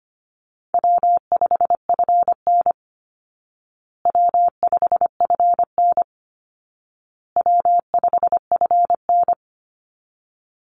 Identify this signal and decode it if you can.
Morse code, heard as a keyed tone, about 25 wpm, 700 Hz. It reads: W5FN W5FN W5FN
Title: Easy Call Signs Repeated 3x - 25wpm